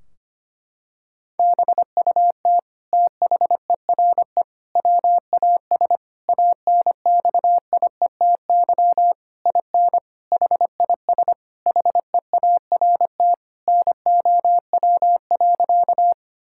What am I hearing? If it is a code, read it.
Morse code, heard as a keyed tone, about 25 wpm, 700 Hz. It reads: BUT THERE WAS ANXIETY IN HIS HEART NOW.
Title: Sentences from Top 1600 Words 25wpm